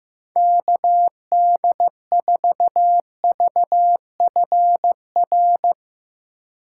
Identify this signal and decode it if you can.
Morse code, heard as a keyed tone, about 16 wpm, 700 Hz. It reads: KD4VFR
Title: Speed Racing - USA 2x3 Call Signs 15wpm